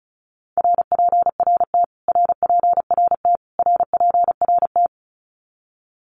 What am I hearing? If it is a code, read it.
Morse code, heard as a keyed tone, about 34 wpm, 700 Hz. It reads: RPRT RPRT RPRT
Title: Abbreviations Repeated 3x - 35wpm